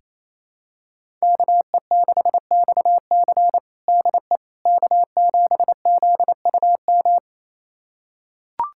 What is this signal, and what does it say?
Morse code, heard as a keyed tone, about 28 wpm, 700 Hz. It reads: KE6XC DE K7ZUM
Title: USA Callsign DE Callsign - 5 Characters - 28wpm